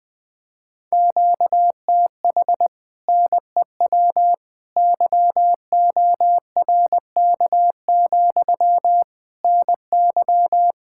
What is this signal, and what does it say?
Morse code, heard as a keyed tone, about 20 wpm, 700 Hz. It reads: QTH NEW YORK, NY
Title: Speed-Racing Common QSO Phrases Encore - 20wpm